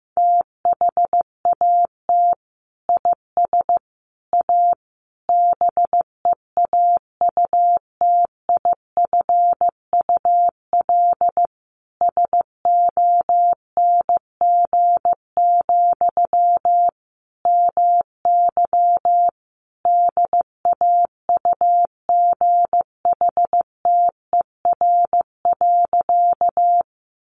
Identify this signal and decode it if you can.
Morse code, heard as a keyed tone, about 15 wpm, 700 Hz. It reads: THAT IS A BEAUTIFUL SONG, MY DAUGHTER.
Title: Sentences from Top 1500 Words 15wpm